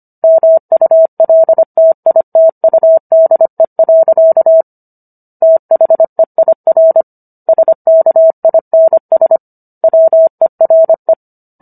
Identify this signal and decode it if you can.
Morse code, heard as a keyed tone, about 25 wpm, 650 Hz. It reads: MULTITUDE. THEIR SKINS WERE